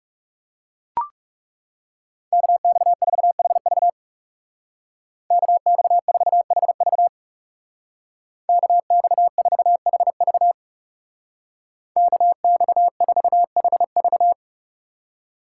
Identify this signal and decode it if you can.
Morse code, heard as a keyed tone, about 36 wpm, 700 Hz. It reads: KX4HV KX4HV KX4HV KX4HV
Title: Speed Racing - USA 2x2 Call Signs 30wpm